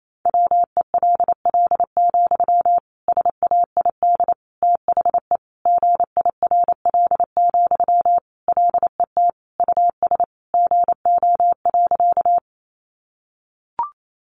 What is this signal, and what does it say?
Morse code, heard as a keyed tone, about 28 wpm, 700 Hz. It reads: WELL, SAID THE GIRL, LET US GO.
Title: Sentences from Top 400 Words 28wpm